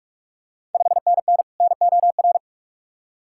Morse code, heard as a keyed tone, about 44 wpm, 690 Hz. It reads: 5NN NOR
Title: Ontario QSO Party Exchange 45wpm